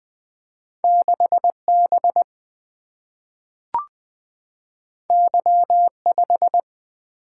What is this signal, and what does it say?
Morse code, heard as a keyed tone, about 20 wpm, 700 Hz. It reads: 6B Y5
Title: Single-Letter Single-Number 20wpm